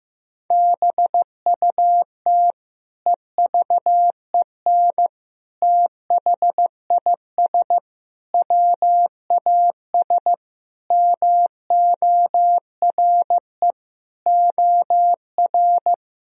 Morse code, heard as a keyed tone, about 15 wpm, 700 Hz. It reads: BUT EVEN THIS WAS MORE OR
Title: Sentences from Top 1000 Words - Review Part 2 - 15wpm